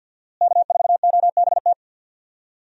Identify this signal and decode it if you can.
Morse code, heard as a keyed tone, about 52 wpm, 700 Hz. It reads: K4OBT
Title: USA Call Signs - 5 Characters - Repeated 3x - 50wpm